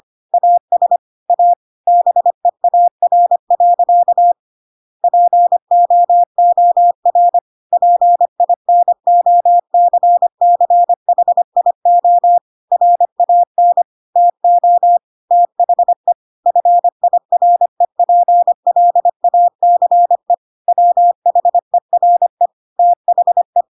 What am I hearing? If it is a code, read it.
Morse code, heard as a keyed tone, about 25 wpm, 700 Hz. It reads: AS A BEAR. POOR PINOCCHIO RAN TO THE FIREPLACE WHERE THE
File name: Pinocchio / Pinocchio ch5 25 25 ik1hge